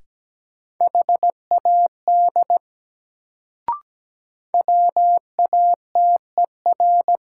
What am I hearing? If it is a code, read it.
Morse code, heard as a keyed tone, about 17 wpm, 700 Hz. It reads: HAD WATER
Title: Sentences from Top 100 Words - New Word Drill - 17wpm